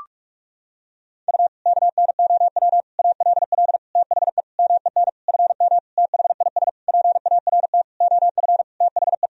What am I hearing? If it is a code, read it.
Morse code, heard as a keyed tone, about 46 wpm, 700 Hz. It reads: U KNOW ALL THE MEN FM THIS PART OF THE
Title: Sentences from Top 300 Words and Abbreviations 45wpm